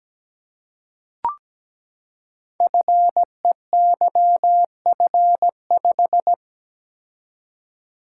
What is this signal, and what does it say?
Morse code, heard as a keyed tone, about 17 wpm, 700 Hz. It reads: FEYF5
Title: Code Groups - 5 Letter:Numbers - 17wpm